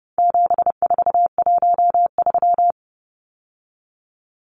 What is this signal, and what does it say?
Morse code, heard as a keyed tone, about 29 wpm, 700 Hz. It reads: 7413
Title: Numbers - 4-Digit 30wpm